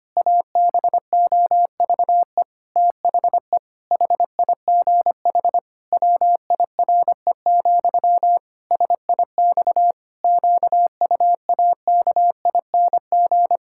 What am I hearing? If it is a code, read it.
Morse code, heard as a keyed tone, about 25 wpm, 700 Hz. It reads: ABOVE THE HIGH WIRE, SIX QUAKING
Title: Pangram Challenge v2 - 25wpm